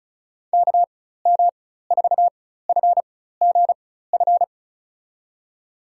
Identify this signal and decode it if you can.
Morse code, heard as a keyed tone, about 36 wpm, spaced as for 16 wpm, 700 Hz. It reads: KM4FGF
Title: USA Call Signs - 6 Characters - 4x Character Spacing - 35wpm